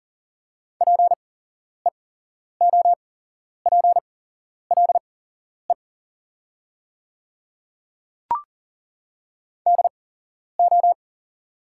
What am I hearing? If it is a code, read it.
Morse code, heard as a keyed tone, about 40 wpm, spaced as for 11 wpm, 690 Hz. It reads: PEOPLE DO
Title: 8x Character Spacing - Top 200 Words - 40wpm